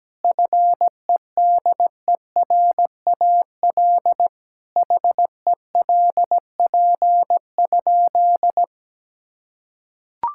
Morse code, heard as a keyed tone, about 17 wpm, 700 Hz. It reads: FEDERAL HELP?
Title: Sentences from Top 1000 Words - Review Part 5 - 17wpm